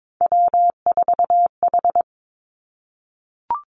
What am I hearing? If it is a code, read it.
Morse code, heard as a keyed tone, about 22 wpm, 700 Hz. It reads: W4H
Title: Three Challenging Letter-Number Combinations 22wpm